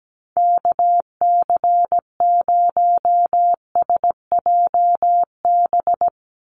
Morse code, heard as a keyed tone, about 17 wpm, 700 Hz. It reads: KC0SJB
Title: USA Call Signs 17wpm